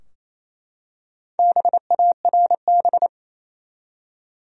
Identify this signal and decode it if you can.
Morse code, heard as a keyed tone, about 28 wpm, 700 Hz. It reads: BARB
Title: USA Common Names - Max 5 Letters - 28wpm